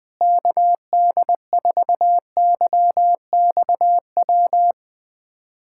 Morse code, heard as a keyed tone, about 19 wpm, 700 Hz. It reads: KD4YXW
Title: USA Callsign DE Callsign 20wpm